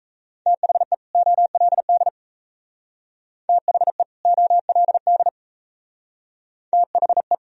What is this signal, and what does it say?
Morse code, heard as a keyed tone, about 35 wpm, 700 Hz. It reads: THE OLD THE OLD THE